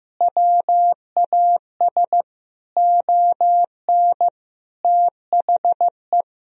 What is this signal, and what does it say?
Morse code, heard as a keyed tone, about 15 wpm, 700 Hz. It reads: WAS ON THE
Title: Sentences from Top 1000 Words - Review Part 1 - 15wpm